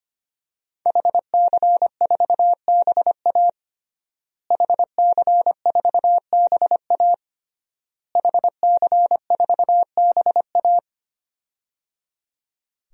Morse code, heard as a keyed tone, about 25 wpm, 700 Hz. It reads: HC4BA HC4BA HC4BA
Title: DXCC-100 Call Signs - 5 Characters - Repeated 3x - 25wpm